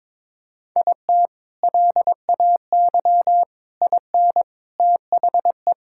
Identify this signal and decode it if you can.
Morse code, heard as a keyed tone, about 22 wpm, 700 Hz. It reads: IT LAY IN THE
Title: Sentences from Top 1000 Words - Review Part 6 - 22wpm